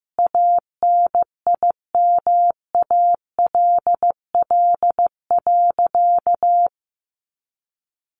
Morse code, heard as a keyed tone, about 15 wpm, 700 Hz. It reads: ANIMALL.
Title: A Sound Of Thunder 15wpm